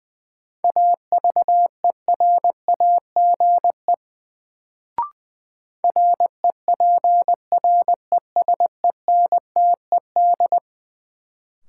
Morse code, heard as a keyed tone, about 20 wpm, 700 Hz. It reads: AVERAGE REPRESENTED
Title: Sets of 5 Words - New Word Drill - 20wpm